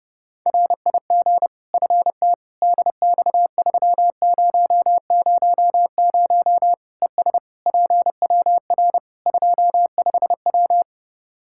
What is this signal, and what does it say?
Morse code, heard as a keyed tone, about 30 wpm, 700 Hz. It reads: RIG FT DX3000 ES PWR 25W
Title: K1USN SST QSO Practice 30wpm